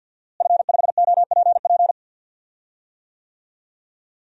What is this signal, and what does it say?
Morse code, heard as a keyed tone, about 51 wpm, 700 Hz. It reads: F5OPP
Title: DXCC-100 Call Signs - 5 Characters - Repeated 2x - 50wpm